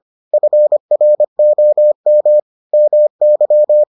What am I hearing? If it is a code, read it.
Morse code, heard as a keyed tone, about 25 wpm, 600 Hz. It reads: FROM MY